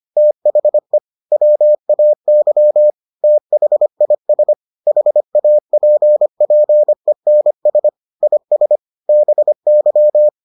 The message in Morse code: THE WAY THIS HAPPENS IS BY

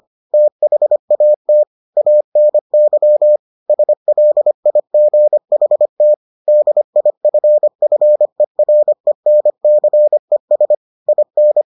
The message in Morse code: THAT ANY SLIGHT DIFFERENCES IN

